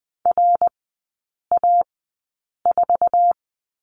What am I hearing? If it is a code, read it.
Morse code, heard as a keyed tone, about 20 wpm, spaced as for 8 wpm, 700 Hz. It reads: RA4
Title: 114 Lesson 19e - Letter w - Three Characters 20wpm